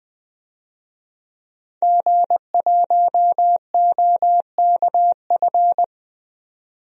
Morse code, heard as a keyed tone, about 20 wpm, 700 Hz. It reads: G1OKF
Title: DXCC-100 Call Signs - 5 Characters - 20wpm